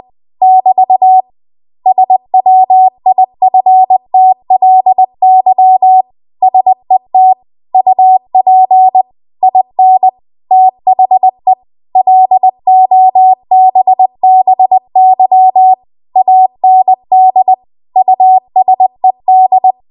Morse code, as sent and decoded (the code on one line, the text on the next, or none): -...- / ... .-- .. ..-. - .-.. -.-- / ... - / ..- .--. / .. -. / - .... / .-.. --- -... -... -.-- / .- -. -.. / ..- ... -..
<BT> SWIFTLY SET UP IN THE LOBBY AND USED